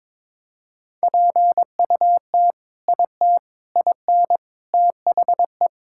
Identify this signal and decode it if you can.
Morse code, heard as a keyed tone, about 21 wpm, 700 Hz. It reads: PUT IT IN THE
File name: practice / 22wpm / Sentences with 2-Word Phrases - 22wpm